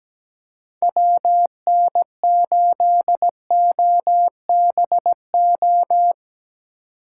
Morse code, heard as a keyed tone, about 17 wpm, 700 Hz. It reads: WN8OBO
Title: USA Call Signs - 6 Characters - 2x Character Spacing - 17wpm